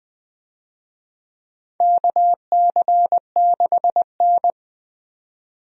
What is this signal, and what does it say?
Morse code, heard as a keyed tone, about 20 wpm, 700 Hz. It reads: KC6N
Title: USA Call Signs - 4 Characters - 20wpm